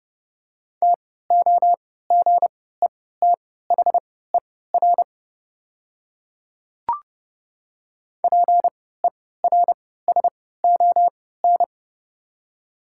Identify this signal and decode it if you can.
Morse code, heard as a keyed tone, about 30 wpm, spaced as for 17 wpm, 700 Hz. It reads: TOGETHER PERSON
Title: 3x Character Spacing - Top 500 Words - 30wpm